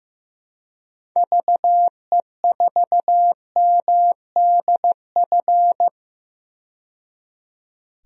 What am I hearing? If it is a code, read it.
Morse code, heard as a keyed tone, about 16 wpm, 700 Hz. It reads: VE4MDF
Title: DXCC-100 Call Signs - 6 Characters - 15wpm